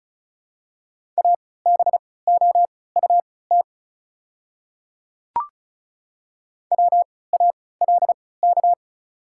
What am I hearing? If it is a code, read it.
Morse code, heard as a keyed tone, about 36 wpm, spaced as for 20 wpm, 700 Hz. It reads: ABOUT WALK
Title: 3x Character Spacing - Top 300 Words - 35wpm